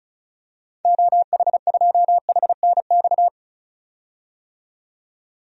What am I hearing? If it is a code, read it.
Morse code, heard as a keyed tone, about 36 wpm, 700 Hz. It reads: OH2HNX
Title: DXCC-100 Call Signs - 6 Characters - 35wpm